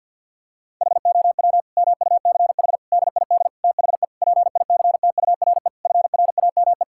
Morse code, heard as a keyed tone, about 50 wpm, 700 Hz. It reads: HOW MUCH DID THE PICTURE FRAME
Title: Sentences from Top 1000 Words - Review Part 6 - 50wpm